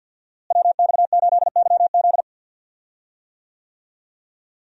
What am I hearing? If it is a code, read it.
Morse code, heard as a keyed tone, about 51 wpm, 700 Hz. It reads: WX8YZ
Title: USA Call Signs - 5 Characters - Repeated 2x - 50wpm